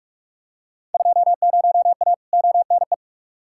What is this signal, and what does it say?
Morse code, heard as a keyed tone, about 42 wpm, 700 Hz. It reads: 20A ONE
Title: Field Day Exchanges Encore - 45wpm